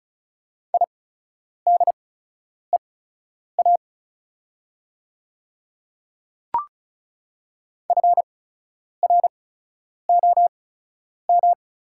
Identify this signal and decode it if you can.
Morse code, heard as a keyed tone, about 36 wpm, spaced as for 10 wpm, 700 Hz. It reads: IDEA FROM
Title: 8x Character Spacing - Top 300 Words - 35wpm